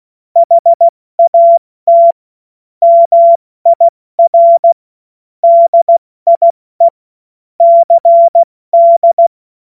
Morse code, heard as a keyed tone, about 16 wpm, 680 Hz. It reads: HAT MIR DIE CD
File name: Morsekurs / Witze / Witz 28. März 2023 70 F80